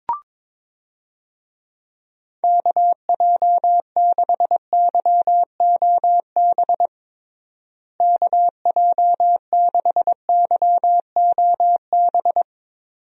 Morse code, heard as a keyed tone, about 22 wpm, 700 Hz. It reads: KJ6YOB KJ6YOB
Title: USA Call Signs - 6 Characters - Repeated 2x - 22wpm